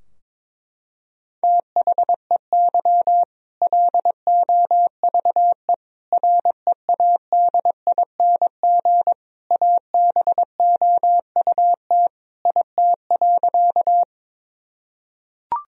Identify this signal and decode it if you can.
Morse code, heard as a keyed tone, about 22 wpm, 700 Hz. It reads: THEY LOVE READING ABOUT IT.
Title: Sentences from Top 1000 Words - Review Part 3 - 22wpm